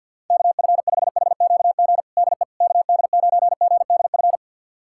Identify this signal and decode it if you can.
Morse code, heard as a keyed tone, about 52 wpm, 700 Hz. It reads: KF5HQG DE KD9GDF
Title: USA Callsign DE Callsign 50wpm